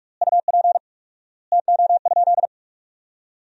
Morse code, heard as a keyed tone, about 46 wpm, 700 Hz. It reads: UP TO?
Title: 4x Word-Spacing - Sentences from Top 500 Words - 45wpm